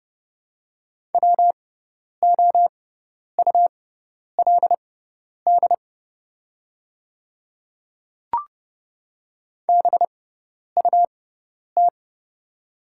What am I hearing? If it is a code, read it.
Morse code, heard as a keyed tone, about 30 wpm, spaced as for 10 wpm, 690 Hz. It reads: WOULD BUT
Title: 6x Character Spacing - Top 100 Words - 30wpm